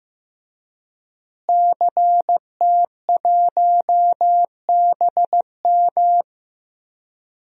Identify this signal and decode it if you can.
Morse code, heard as a keyed tone, about 15 wpm, 700 Hz. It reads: CT1BM
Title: DXCC-100 Call Signs - 5 Characters - Repeated 2x - 15wpm